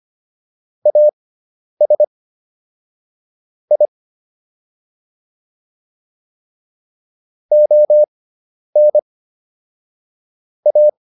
Morse code, heard as a keyed tone, about 25 wpm, spaced as for 10 wpm, 600 Hz. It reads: AS I ON A